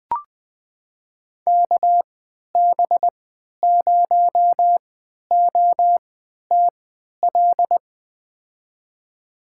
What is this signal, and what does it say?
Morse code, heard as a keyed tone, about 20 wpm, spaced as for 11 wpm, 700 Hz. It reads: KB0OTL
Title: USA Call Signs - 6 Characters - 3x Character Spacing - 20wpm